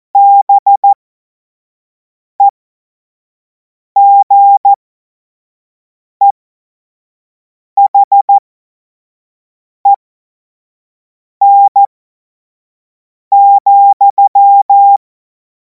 Morse code, heard as a keyed tone, about 14 wpm, spaced as for 5 wpm, 800 Hz. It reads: BEGEHEN,
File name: Morsekurs / Witze / Witz 24. März 2026 25 F70